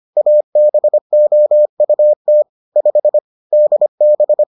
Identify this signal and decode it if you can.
Morse code, heard as a keyed tone, about 25 wpm, 600 Hz. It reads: ABOUT 5 DB